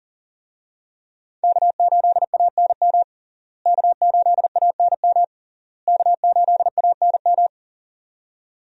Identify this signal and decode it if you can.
Morse code, heard as a keyed tone, about 40 wpm, 700 Hz. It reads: K8ANM K8ANM K8ANM
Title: USA Call Signs - 5 Characters - Repeated 3x - 40wpm